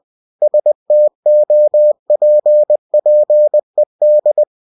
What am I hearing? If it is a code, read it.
Morse code, heard as a keyed tone, about 20 wpm, 600 Hz. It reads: STOPPED